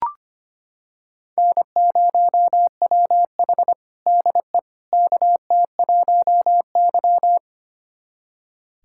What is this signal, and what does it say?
Morse code, heard as a keyed tone, about 25 wpm, 700 Hz. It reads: N0WH DE KT1Y
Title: USA Callsign DE Callsign - 4 Characters - 25wpm